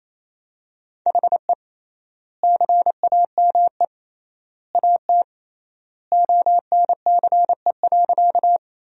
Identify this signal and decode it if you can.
Morse code, heard as a keyed tone, about 28 wpm, 700 Hz. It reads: HE CAME AT ONCE.
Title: 3x Word-Spacing - Sentences from Top 300 Words - 28wpm